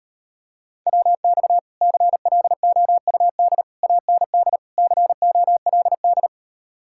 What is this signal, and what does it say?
Morse code, heard as a keyed tone, about 38 wpm, 700 Hz. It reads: WX CLOUD AND COLD